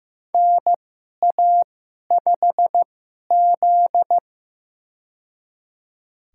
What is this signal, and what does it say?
Morse code, heard as a keyed tone, about 15 wpm, spaced as for 11 wpm, 700 Hz. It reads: NA5Z
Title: USA Call Signs - 2x Character Spacing - 15wpm